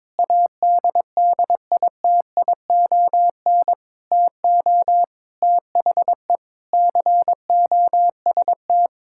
Morse code, heard as a keyed tone, about 22 wpm, 700 Hz. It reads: ADDITION TO THE COST
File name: practice / 22wpm / Sentences with 4-Word Phrases - 22wpm